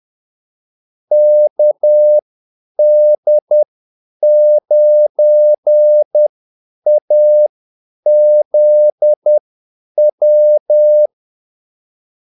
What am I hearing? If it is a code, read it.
Morse code, heard as a keyed tone, about 10 wpm, spaced as for 8 wpm, 600 Hz. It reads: KD9AZW
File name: Callsigns-10-8wpm